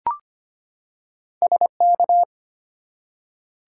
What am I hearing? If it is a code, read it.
Morse code, heard as a keyed tone, about 25 wpm, 700 Hz. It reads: SK
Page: morse-code-ninja-practice-25wpm